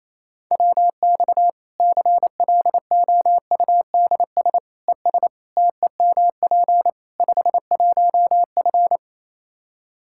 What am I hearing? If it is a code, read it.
Morse code, heard as a keyed tone, about 28 wpm, 700 Hz. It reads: WX CLOUDS ES TEMP 51F
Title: Practice QSO - Exchange 2 - Info - 28wpm